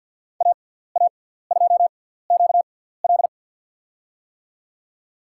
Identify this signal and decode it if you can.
Morse code, heard as a keyed tone, about 53 wpm, spaced as for 17 wpm, 700 Hz. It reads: AA2QL
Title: USA Call Signs - 5 Characters - 6x Character Spacing - 50wpm